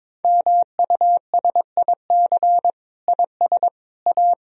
Morse code, heard as a keyed tone, about 21 wpm, 700 Hz. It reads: MUSIC IS A
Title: Sentences from Top 300 Words and Abbreviations 22wpm